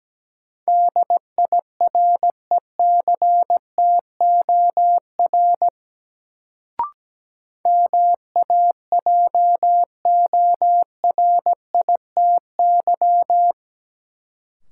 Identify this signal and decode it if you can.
Morse code, heard as a keyed tone, about 17 wpm, 700 Hz. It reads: DIRECTOR MAJORITY
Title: Sets of 3 Words - New Word Drill - 17wpm